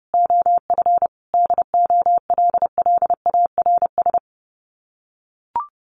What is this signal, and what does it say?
Morse code, heard as a keyed tone, about 30 wpm, 700 Hz. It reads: OF DOLLARS
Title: Sets of 5 Words 30wpm